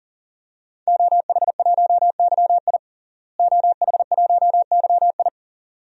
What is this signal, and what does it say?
Morse code, heard as a keyed tone, about 40 wpm, 700 Hz. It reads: OH1YI OH1YI
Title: DXCC-100 Call Signs - 5 Characters - Repeated 2x - 40wpm